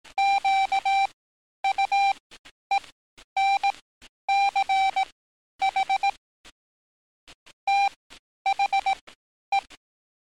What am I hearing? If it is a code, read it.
Morse code, heard as a keyed tone, about 18 wpm, spaced as for 10 wpm, 780 Hz. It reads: QUENCH THE
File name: training / NAVPERS 11049to11060 / NAVPERS 11050 RA RB